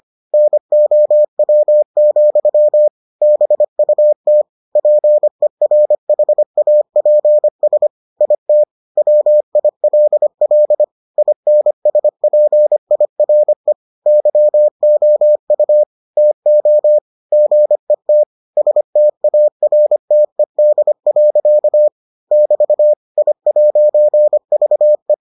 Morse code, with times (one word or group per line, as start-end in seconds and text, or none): 0.3-2.9 NOW,
3.2-4.4 BUT
4.8-7.9 PERHAPS
8.2-8.6 IT
9.0-10.8 WILL
11.2-13.7 INSPIRE
14.1-15.8 YOU
16.2-17.0 TO
17.3-18.2 GET
18.6-21.9 STARTED.
22.3-22.9 <BT>
23.2-25.1 I'VE